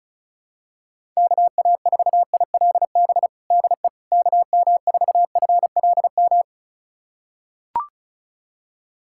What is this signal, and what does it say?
Morse code, heard as a keyed tone, about 36 wpm, 700 Hz. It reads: KA4ILB DE KM4FLM
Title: USA Callsign DE Callsign 35wpm